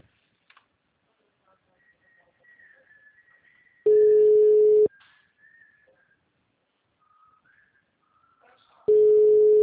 Ringback tone (Europe, 425 Hz).
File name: out- -2013- - - .10993